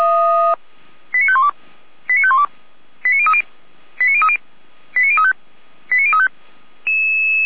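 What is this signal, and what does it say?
Five-tone selective call (ZVEI), heard as quick runs of five tones, tones of about 70 ms. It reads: 89512 89512 89030 89030 89035 89035
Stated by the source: ZVEI five-tone call received by SDR.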